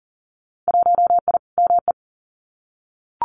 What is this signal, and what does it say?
Morse code, heard as a keyed tone, about 40 wpm, 700 Hz. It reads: 1I ME
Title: Winter Field Day Exchanges Encore 40wpm